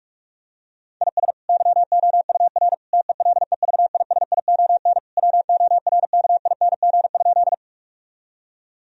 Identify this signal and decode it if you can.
Morse code, heard as a keyed tone, about 46 wpm, 700 Hz. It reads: IS YOUR TELEVISION WORKING?
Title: Sentences from Top 1000 Words - Review Part 4 - 45wpm